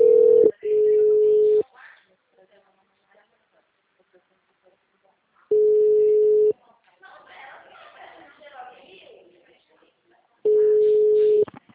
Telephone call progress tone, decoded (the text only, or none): ringback tone